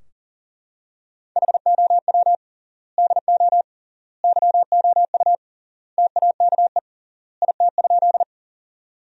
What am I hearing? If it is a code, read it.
Morse code, heard as a keyed tone, about 40 wpm, 700 Hz. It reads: HOW DO YOU TAKE IT?
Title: 3x Word-Spacing - Sentences from Top 200 Words - 40wpm